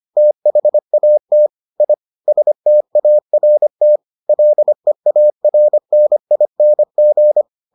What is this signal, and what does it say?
Morse code, heard as a keyed tone, about 25 wpm, 600 Hz. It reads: THAT I START LEARNING